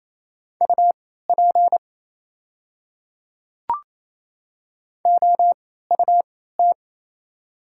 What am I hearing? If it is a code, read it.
Morse code, heard as a keyed tone, about 27 wpm, spaced as for 16 wpm, 700 Hz. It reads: UP OUT
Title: 3x Character Spacing - Top 500 Words - 28wpm